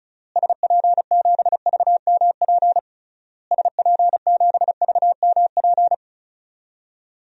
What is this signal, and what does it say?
Morse code, heard as a keyed tone, about 36 wpm, 700 Hz. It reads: SP7VMP SP7VMP
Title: DXCC-100 Call Signs - 6 Characters - Repeated 3x - 35wpm